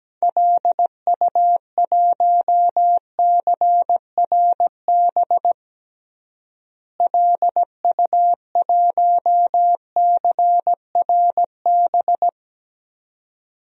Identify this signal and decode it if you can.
Morse code, heard as a keyed tone, about 17 wpm, 700 Hz. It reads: LU1CRB LU1CRB
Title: DXCC-100 Call Signs - 6 Characters - Repeated 2x - 17wpm